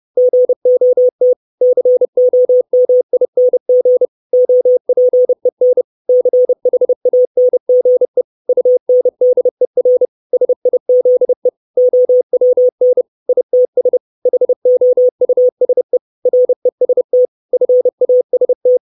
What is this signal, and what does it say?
Morse code, heard as a keyed tone, about 30 wpm, 500 Hz. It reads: GOT COMING OPEN CHANGE UNDER SIZE OWN ITS HOUSE REST FAST
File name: 632 words / 632 Words 30WPM